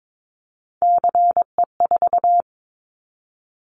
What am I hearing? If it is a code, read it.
Morse code, heard as a keyed tone, about 22 wpm, 700 Hz. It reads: CE4
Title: Callsign Prefixes for Top 100 DXCC - 22wpm